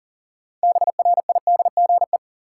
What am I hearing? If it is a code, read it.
Morse code, heard as a keyed tone, about 40 wpm, 700 Hz. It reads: BRIDGE
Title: Sentences from Top 1200 Words - New Word Drill - 40wpm